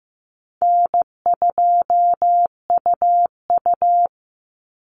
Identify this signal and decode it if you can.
Morse code, heard as a keyed tone, about 15 wpm, 700 Hz. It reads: N2UU
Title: Call Signs Repeated 3x - 15wpm